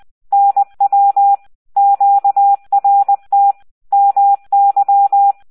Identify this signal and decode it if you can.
Morse code, heard as a keyed tone, about 20 wpm, 800 Hz. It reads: NW QRT MY